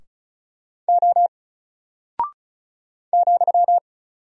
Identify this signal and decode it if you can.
Morse code, heard as a keyed tone, about 36 wpm, 700 Hz. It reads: O ,
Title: Single Character (Letter:Number:Punctuation:Prosign) 35wpm